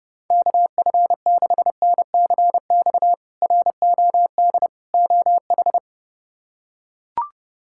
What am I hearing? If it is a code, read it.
Morse code, heard as a keyed tone, about 31 wpm, 700 Hz. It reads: KF6NCX ROD OH
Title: K1USN SST QSO Practice - 30wpm